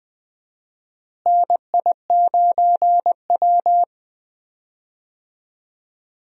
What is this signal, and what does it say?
Morse code, heard as a keyed tone, about 20 wpm, 700 Hz. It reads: NI9W
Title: USA Call Signs - 4 Characters - Repeated 3x - 20wpm